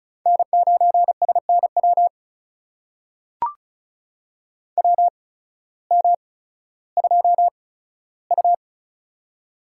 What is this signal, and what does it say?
Morse code, heard as a keyed tone, about 36 wpm, spaced as for 16 wpm, 700 Hz. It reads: N9SNW WM2U
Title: USA Call Signs - 8x Character Spacing - 35wpm